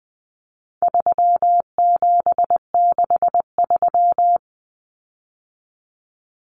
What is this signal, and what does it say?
Morse code, heard as a keyed tone, about 20 wpm, 700 Hz. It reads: 3763
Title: Common Cut Numbers - 4-Digit 20wpm